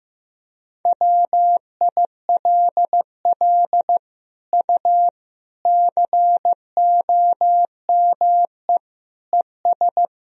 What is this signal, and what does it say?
Morse code, heard as a keyed tone, about 15 wpm, 700 Hz. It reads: WILL U COME ES
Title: Sentences from Top 100 Words and Abbreviations 15wpm